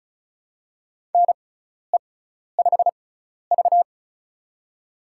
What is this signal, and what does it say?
Morse code, heard as a keyed tone, about 34 wpm, spaced as for 12 wpm, 700 Hz. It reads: NE5V
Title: USA Call Signs - 4 Characters - 6x Character Spacing - 35wpm